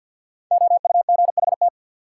Morse code, heard as a keyed tone, about 52 wpm, 700 Hz. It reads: OUGHT
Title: Sets of 5 Words Encore - New Word Drill - 50wpm